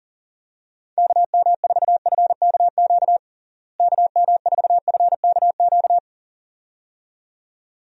Morse code, heard as a keyed tone, about 40 wpm, 700 Hz. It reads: KM4FKQ KM4FKQ
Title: USA Call Signs - 6 Characters - Repeated 2x - 40wpm